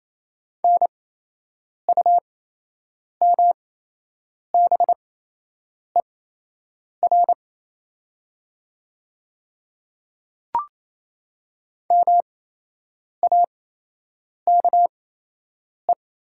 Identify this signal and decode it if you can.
Morse code, heard as a keyed tone, about 28 wpm, spaced as for 8 wpm, 700 Hz. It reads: NUMBER MAKE
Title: 8x Character Spacing - Top 300 Words - 28wpm